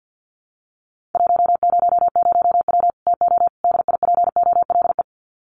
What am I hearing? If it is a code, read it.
Morse code, heard as a keyed tone, about 50 wpm, 700 Hz. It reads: 100W TO DIPOLE
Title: Common QSO Phrases 50wpm